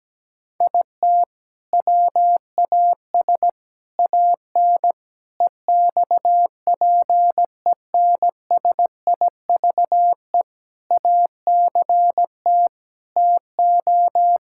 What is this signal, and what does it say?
Morse code, heard as a keyed tone, about 17 wpm, 700 Hz. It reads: IT WAS AN EXPENSIVE ACT TO
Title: Sentences from Top 1000 Words - Review Part 6 - 17wpm